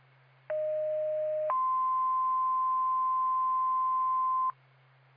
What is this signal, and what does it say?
Two-tone page, heard as a one-second tone, then a three-second tone, about 620 then 1040 Hz. It is heard from 0.5 to 4.5 s.